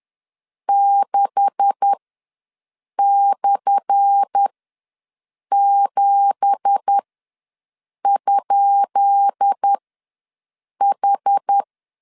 Morse code, heard as a keyed tone, about 10 wpm, spaced as for 6 wpm, 800 Hz. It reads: 6/7?H
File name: LEZIONI / 10 SIGNS / 10B INT ws 10wpm 4134